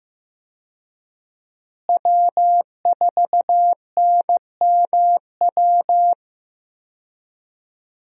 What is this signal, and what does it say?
Morse code, heard as a keyed tone, about 15 wpm, 700 Hz. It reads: W4NMW
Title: USA Call Signs - 5 Characters - Repeated 3x - 15wpm